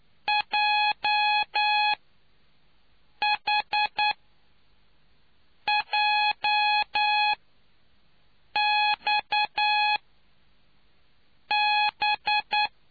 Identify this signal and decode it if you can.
Morse code, heard as a keyed tone, about 9 wpm, spaced as for 5 wpm, 840 Hz. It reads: JHJXB